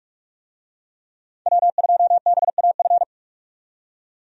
Morse code, heard as a keyed tone, about 41 wpm, 700 Hz. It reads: W2BAF
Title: USA Call Signs - 5 Characters - 45wpm